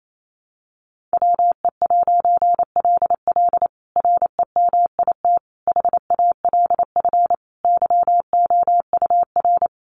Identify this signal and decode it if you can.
Morse code, heard as a keyed tone, about 28 wpm, 700 Hz. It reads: WE'LL REMIT HALF YOUR
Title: A Sound Of Thunder 28wpm